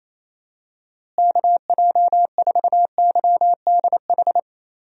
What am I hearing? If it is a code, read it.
Morse code, heard as a keyed tone, about 27 wpm, 700 Hz. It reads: KJ4YDH
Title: USA Call Signs - 6 Characters - Repeated 2x - 28wpm